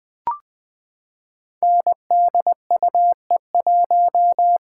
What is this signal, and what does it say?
Morse code, heard as a keyed tone, about 21 wpm, 700 Hz. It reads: NDUE1
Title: Code Groups - 5 Letter:Numbers - 20wpm